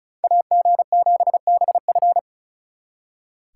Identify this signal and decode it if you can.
Morse code, heard as a keyed tone, about 46 wpm, 700 Hz. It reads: AG7BF